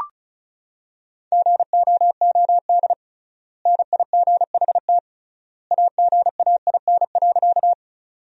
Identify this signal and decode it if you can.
Morse code, heard as a keyed tone, about 36 wpm, 700 Hz. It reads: GOOD NIGHT AGAIN.
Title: 3x Word-Spacing - Sentences from Top 300 Words - 35wpm